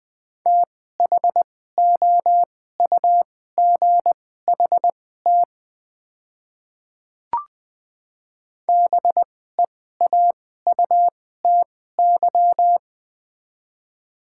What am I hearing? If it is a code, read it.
Morse code, heard as a keyed tone, about 20 wpm, spaced as for 14 wpm, 700 Hz. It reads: THOUGHT BEAUTY
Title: 2x Character Spacing - Top 500 Words - 20wpm